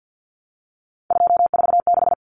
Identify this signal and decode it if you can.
Morse code, heard as a keyed tone, about 50 wpm, 700 Hz. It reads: 246
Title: Numbers - 3-Digit 50wpm